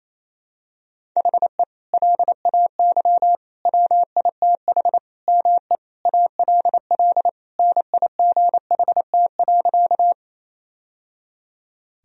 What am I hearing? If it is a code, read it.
Morse code, heard as a keyed tone, about 28 wpm, 690 Hz. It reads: HE LAY WITH ME ALL NIGHT.
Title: Sentences from Top 1000 Words - Review Part 2 - 28wpm